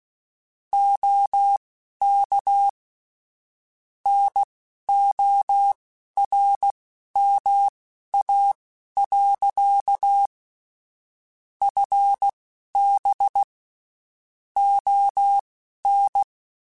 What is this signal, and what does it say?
Morse code, heard as a keyed tone, about 16 wpm, spaced as for 11 wpm, 800 Hz. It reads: OK NORMA. FB ON
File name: QSOs / QSOs-mp3s / WD0FBT 1@15